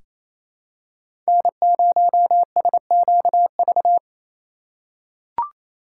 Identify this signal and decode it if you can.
Morse code, heard as a keyed tone, about 28 wpm, 700 Hz. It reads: N0SQV
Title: USA Call Signs - 5 Characters - 2x Character Spacing - 28wpm